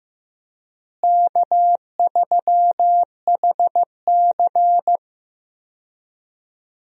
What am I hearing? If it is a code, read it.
Morse code, heard as a keyed tone, about 16 wpm, 700 Hz. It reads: K3HC
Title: USA Call Signs - 4 Characters - Repeated 3x - 15wpm